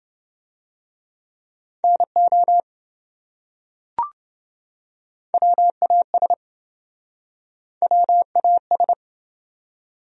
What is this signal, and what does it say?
Morse code, heard as a keyed tone, about 27 wpm, 700 Hz. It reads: NO WAS WAS